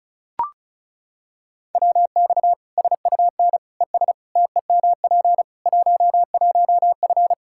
Morse code, heard as a keyed tone, about 36 wpm, 700 Hz. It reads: WX SUN ES TEMP 11F
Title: Practice QSO - Exchange 2 - Info - 35wpm